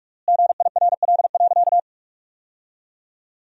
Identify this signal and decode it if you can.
Morse code, heard as a keyed tone, about 41 wpm, 700 Hz. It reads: GIRL.
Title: 3x Word-Spacing - Sentences from Top 400 Words - 45wpm